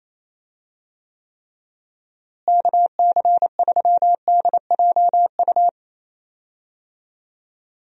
Morse code, heard as a keyed tone, about 28 wpm, 700 Hz. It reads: KC3DJU
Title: USA Call Signs - 6 Characters - Repeated 3x - 28wpm